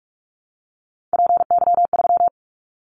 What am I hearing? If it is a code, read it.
Morse code, heard as a keyed tone, about 46 wpm, 700 Hz. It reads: PY3
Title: Callsign Prefixes for Top 100 DXCC - 45wpm